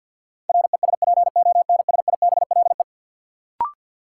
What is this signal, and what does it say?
Morse code, heard as a keyed tone, about 53 wpm, 700 Hz. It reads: RESPONSIBLE
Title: Sentences from Top 1000 Words - Review Part 7 - New Word Drill - 50wpm